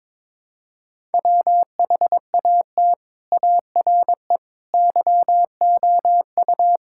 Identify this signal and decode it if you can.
Morse code, heard as a keyed tone, about 21 wpm, 700 Hz. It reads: WHAT ARE YOU
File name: practice / 22wpm / Sentences with 3-Word Phrases - 22wpm